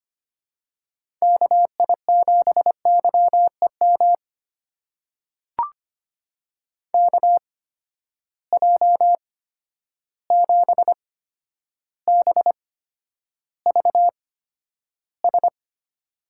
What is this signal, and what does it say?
Morse code, heard as a keyed tone, about 25 wpm, spaced as for 11 wpm, 700 Hz. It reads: KI7YEM KJ7BVS
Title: USA Call Signs - 6 Characters - 8x Character Spacing - 25wpm